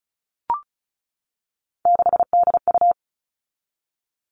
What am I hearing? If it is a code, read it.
Morse code, heard as a keyed tone, about 34 wpm, 700 Hz. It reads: 6DU
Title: All Cut Numbers - 3-Digit 35wpm